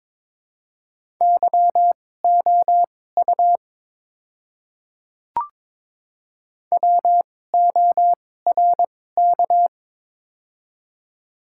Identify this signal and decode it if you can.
Morse code, heard as a keyed tone, about 22 wpm, spaced as for 16 wpm, 700 Hz. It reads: YOU WORK
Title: 2x Character Spacing - Top 200 Words - 22wpm